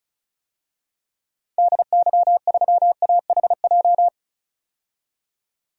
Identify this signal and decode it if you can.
Morse code, heard as a keyed tone, about 34 wpm, 700 Hz. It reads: DY3AHJ
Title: DXCC-100 Call Signs - 6 Characters - 35wpm